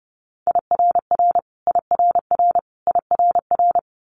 Morse code, heard as a keyed tone, about 30 wpm, 690 Hz. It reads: IRR IRR IRR
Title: Multi-letter Phonemes Repeated 3x 30wpm